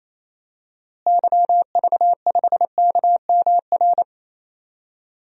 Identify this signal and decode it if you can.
Morse code, heard as a keyed tone, about 28 wpm, 700 Hz. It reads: YV5KMR
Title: DXCC-100 Call Signs - 6 Characters - Repeated 3x - 28wpm